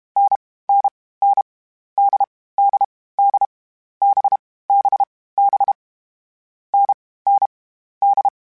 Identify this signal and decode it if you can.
Morse code, heard as a keyed tone, about 32 wpm, 800 Hz. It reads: N N N D D D B B B N N D
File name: Variable Speed / Variable Speed-mp3s / GZ GZ7 sp @30